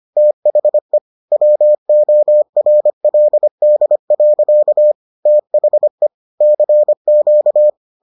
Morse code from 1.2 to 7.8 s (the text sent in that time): WORLD. THE CQ